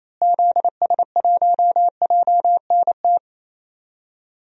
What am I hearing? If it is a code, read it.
Morse code, heard as a keyed tone, about 28 wpm, 700 Hz. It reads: ZS1JNT
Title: DXCC-100 Call Signs - 6 Characters - Repeated 3x - 28wpm